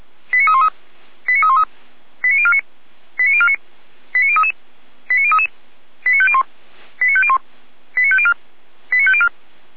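Five-tone selective call (ZVEI), heard as quick runs of five tones, tones of about 70 ms. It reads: 89313 89313 89049 89049 89033 89033 89501 89501 89504 89504
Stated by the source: Five-tone ZVEI selective call received on an SDR.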